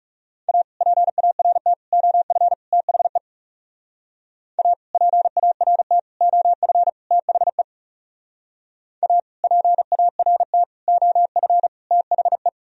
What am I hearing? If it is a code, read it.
Morse code, heard as a keyed tone, about 36 wpm, 690 Hz. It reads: A PART OF THE A PART OF THE A PART OF THE
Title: Speed-Racing - Sets of 4 Words 30wpm